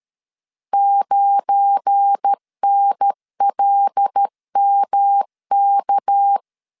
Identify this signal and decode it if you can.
Morse code, heard as a keyed tone, about 13 wpm, 800 Hz. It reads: 9NLMK
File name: CW / DET / 1 9 1500C S B 12wpm 3819